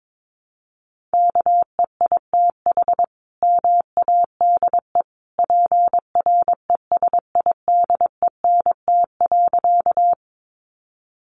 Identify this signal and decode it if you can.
Morse code, heard as a keyed tone, about 22 wpm, 700 Hz. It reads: KEITH MADE PRESIDENT.
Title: A Sound Of Thunder 22wpm